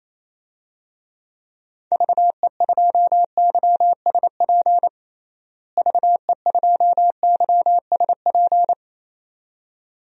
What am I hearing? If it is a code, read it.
Morse code, heard as a keyed tone, about 27 wpm, 700 Hz. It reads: VE2YSP VE2YSP
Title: DXCC-100 Call Signs - 6 Characters - Repeated 2x - 28wpm